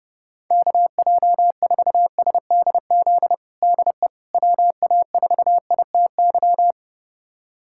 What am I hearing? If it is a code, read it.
Morse code, heard as a keyed tone, about 32 wpm, 690 Hz. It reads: KJ4SDZ DE WA4ITY
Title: USA Callsign DE Callsign - 6 Characters - 30wpm